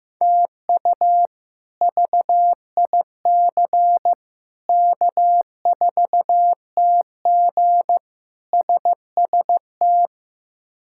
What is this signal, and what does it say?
Morse code, heard as a keyed tone, about 15 wpm, 700 Hz. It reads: TU VIC K4TG SST
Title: K1USN SST QSO Practice - 15wpm